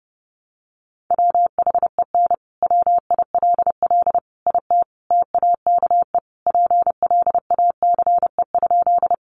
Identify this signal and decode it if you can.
Morse code, heard as a keyed tone, about 30 wpm, 700 Hz. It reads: WHEN WILL IT TAKE PLACE?
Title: Sentences from Top 200 Words 30wpm